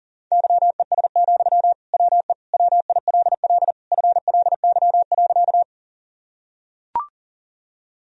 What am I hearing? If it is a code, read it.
Morse code, heard as a keyed tone, about 37 wpm, 700 Hz. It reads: YES, WE WILL FLY.
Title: Sentences from Top 1000 Words - Review Part 2 - 40wpm